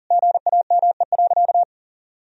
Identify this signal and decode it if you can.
Morse code, heard as a keyed tone, about 40 wpm, 700 Hz. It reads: GAME.
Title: Sentences from Top 1000 Words - Review Part 4 - 40wpm